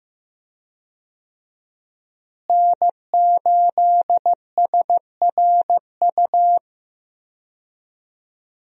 Morse code, heard as a keyed tone, about 15 wpm, 700 Hz. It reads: N8SRU
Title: USA Call Signs - 5 Characters - Repeated 2x - 15wpm